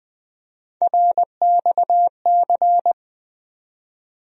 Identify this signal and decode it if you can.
Morse code, heard as a keyed tone, about 20 wpm, 700 Hz. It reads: RXC
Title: USA Call Sign Postfixes - 3 Letters 20wpm